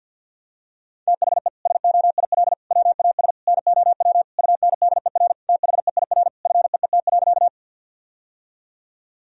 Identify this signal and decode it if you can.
Morse code, heard as a keyed tone, about 50 wpm, 700 Hz. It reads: THE SOIL WAS NOW UNDER THEIR FEET.
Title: Sentences from Top 1000 Words - Review Part 4 - 50wpm